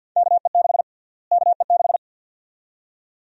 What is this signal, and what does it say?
Morse code, heard as a keyed tone, about 52 wpm, 700 Hz. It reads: KE6 KE6
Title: USA Call Sign Prefixes - Repeated 2x - 50wpm